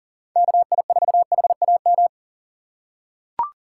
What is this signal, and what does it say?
Morse code, heard as a keyed tone, about 37 wpm, 690 Hz. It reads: KI4HAM
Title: USA Call Signs - 6x Character Spacing - 40wpm